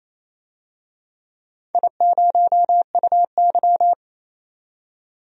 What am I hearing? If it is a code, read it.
Morse code, heard as a keyed tone, about 27 wpm, 690 Hz. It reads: I0UY
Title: DXCC-100 Call Signs - 4 Characters - 28wpm